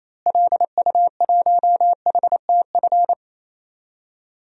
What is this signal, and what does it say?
Morse code, heard as a keyed tone, about 28 wpm, 700 Hz. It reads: LU1HTF
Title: DXCC-100 Call Signs - 6 Characters - Repeated 2x - 28wpm